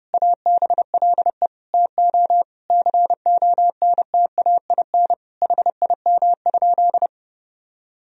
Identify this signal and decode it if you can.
Morse code, heard as a keyed tone, about 30 wpm, 700 Hz. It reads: ABLE TO CONTAIN HIM?
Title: Sentences from Top 1000 Words - Review Part 2 - 30wpm